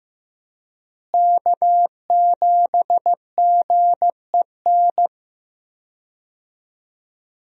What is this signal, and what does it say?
Morse code, heard as a keyed tone, about 15 wpm, 700 Hz. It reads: K7GEN
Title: USA Call Signs - 5 Characters - Repeated 3x - 15wpm